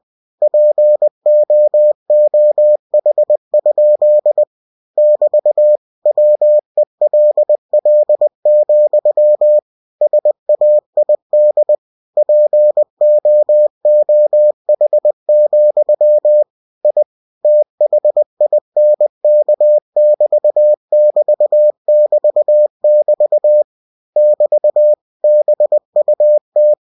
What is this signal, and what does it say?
Morse code, heard as a keyed tone, about 20 wpm, 600 Hz. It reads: POOH? <BT> WELL, SAID POOH, I THINK<BT><BT><BT><BT> <BT> BUT
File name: cw-books / winnie-the-pooh / winnie-20wpm-page-0085